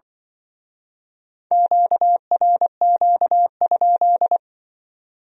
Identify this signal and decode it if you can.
Morse code, heard as a keyed tone, about 25 wpm, 700 Hz. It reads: QRQ?
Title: Speed-Racing Q-Codes - 20wpm